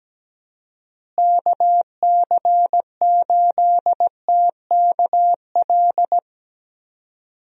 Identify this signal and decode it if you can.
Morse code, heard as a keyed tone, about 17 wpm, 700 Hz. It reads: KC8TKL
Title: USA Call Signs - 6 Characters - Repeated 3x - 17wpm